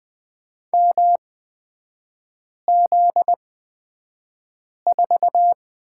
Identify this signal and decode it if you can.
Morse code, heard as a keyed tone, about 20 wpm, spaced as for 5 wpm, 700 Hz. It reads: MZ4
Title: 255 Lesson 41e - Period - Three Characters 20wpm